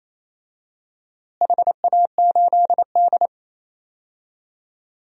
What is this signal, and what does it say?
Morse code, heard as a keyed tone, about 28 wpm, 700 Hz. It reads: HA8D
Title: DXCC-100 Call Signs - 4 Characters - Repeated 3x - 28wpm